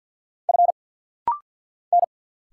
Morse code, heard as a keyed tone, about 50 wpm, 690 Hz. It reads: F N
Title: Single Letter-Number Alt - Rapid-Fire 50wpm